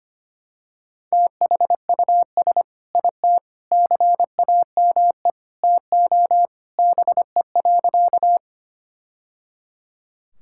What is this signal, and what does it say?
Morse code, heard as a keyed tone, about 25 wpm, 700 Hz. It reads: THUS IT CAME TO BE.
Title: Sentences from Top 1000 Words - Review Part 5 - 25wpm